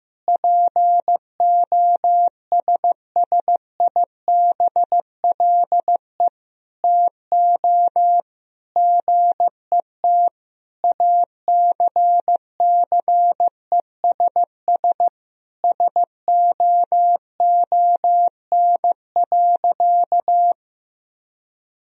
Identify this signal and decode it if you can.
Morse code, heard as a keyed tone, about 15 wpm, 700 Hz. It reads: POSSIBLE TO GET ACCESS SOON.
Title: Sentences from Top 1000 Words - Review Part 4 - 15wpm